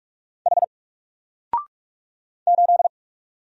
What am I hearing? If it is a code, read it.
Morse code, heard as a keyed tone, about 46 wpm, 700 Hz. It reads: H 8
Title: Single Letter-Number Alt 45wpm